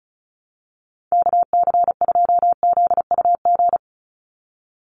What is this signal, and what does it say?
Morse code, heard as a keyed tone, about 34 wpm, 700 Hz. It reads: KC2ZUG
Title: USA Call Signs 35wpm